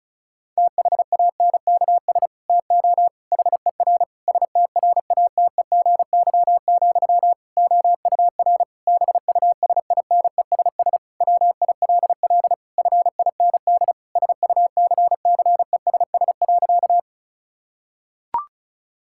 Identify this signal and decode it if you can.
Morse code, heard as a keyed tone, about 36 wpm, 700 Hz. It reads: THANKS TO HER STRATEGY, OUR BUSINESS WILL FIND SUCCESS.
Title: Sentences from Top 1000 Words - Review Part 5 - 35wpm